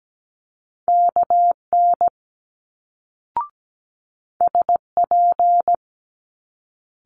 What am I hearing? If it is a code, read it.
Morse code, heard as a keyed tone, about 17 wpm, 700 Hz. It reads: KN SP
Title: Multi-letter Phonemes Quick Fire 17wpm